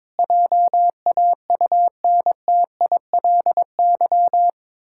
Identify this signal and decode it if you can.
Morse code, heard as a keyed tone, about 22 wpm, 700 Hz. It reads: JAUNTILY
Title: Pangram Challenge v2 - 22wpm